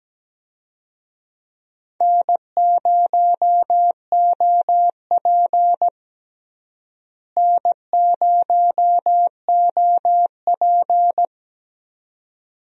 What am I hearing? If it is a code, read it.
Morse code, heard as a keyed tone, about 17 wpm, 700 Hz. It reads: N0OP N0OP
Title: USA Call Signs - 4 Characters - Repeated 2x - 17wpm